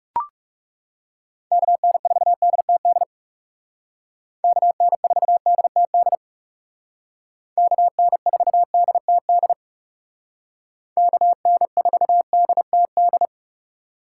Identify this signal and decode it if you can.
Morse code, heard as a keyed tone, about 40 wpm, 700 Hz. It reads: KN4DTD KN4DTD KN4DTD KN4DTD